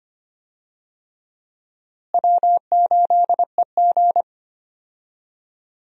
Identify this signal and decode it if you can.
Morse code, heard as a keyed tone, about 25 wpm, 700 Hz. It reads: W8EG
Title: USA Call Signs - 4 Characters - Repeated 3x - 25wpm